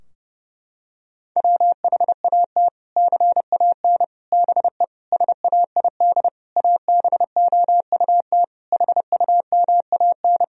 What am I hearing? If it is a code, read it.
Morse code, heard as a keyed tone, about 30 wpm, 700 Hz. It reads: WHAT CAN BE SAID ABOUT HUMAN
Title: Sentences from Top 1000 Words - Review Part 4 - 30wpm